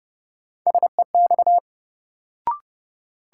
Morse code, heard as a keyed tone, about 30 wpm, 700 Hz. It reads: SEX
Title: Sentences from Top 1600 Words - New Word Drill Alt - 30wpm